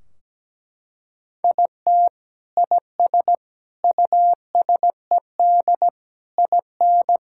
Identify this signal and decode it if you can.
Morse code, heard as a keyed tone, about 17 wpm, 700 Hz. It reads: IT IS USED IN
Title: Sentences from Top 1600 Words 17wpm